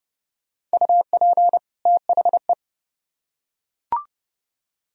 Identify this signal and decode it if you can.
Morse code, heard as a keyed tone, about 28 wpm, 700 Hz. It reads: UP THE